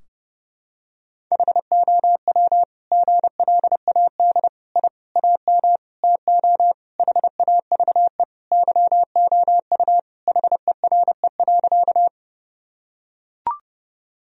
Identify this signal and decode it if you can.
Morse code, heard as a keyed tone, about 32 wpm, 700 Hz. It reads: HOW GLAD I AM TO HAVE YOU HERE.
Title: Sentences from Top 1000 Words - Review Part 2 - 30wpm